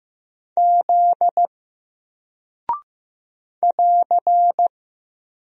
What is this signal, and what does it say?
Morse code, heard as a keyed tone, about 15 wpm, 700 Hz. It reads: Z <AR>
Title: Single Character (Letter:Number:Punctuation:Prosign) 15wpm